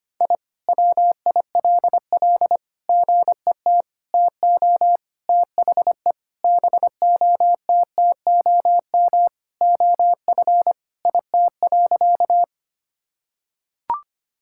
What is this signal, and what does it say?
Morse code, heard as a keyed tone, about 25 wpm, 700 Hz. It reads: I WILL GET TO THE BOTTOM OF IT.
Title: Sentences from Top 1000 Words - Review Part 6 - 25wpm